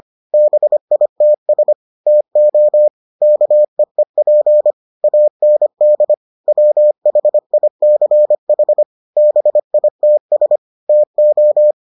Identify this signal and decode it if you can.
Morse code, heard as a keyed tone, about 25 wpm, 600 Hz. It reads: BITS TO KEEP AND WHICH BITS TO